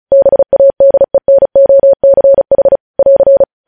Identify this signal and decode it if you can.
Morse code, heard as a keyed tone, about 36 wpm, 550 Hz. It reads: BADENOCH <AR>
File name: podcast / News headlines-1x-35x35